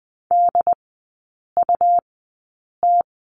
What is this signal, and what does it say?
Morse code, heard as a keyed tone, about 20 wpm, spaced as for 8 wpm, 700 Hz. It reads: DUT
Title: 138 Lesson 23e - Letter f - Three Characters 20wpm